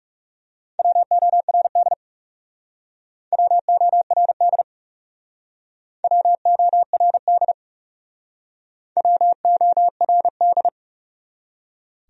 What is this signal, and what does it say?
Morse code, heard as a keyed tone, about 36 wpm, 700 Hz. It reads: WORD WORD WORD WORD
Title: Speed-Racing Top 100 Words 30wpm